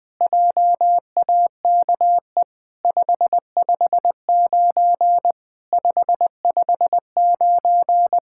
Morse code, heard as a keyed tone, about 20 wpm, 700 Hz. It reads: JAKE 559 559
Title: SOTA QSO Practice 20wpm